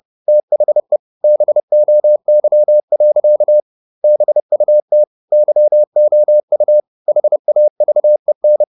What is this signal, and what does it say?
Morse code, heard as a keyed tone, about 30 wpm, 600 Hz. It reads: THE BOY. BUT YOU HAVEN
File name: the old man and the sea / the old man and the sea 1 30wpm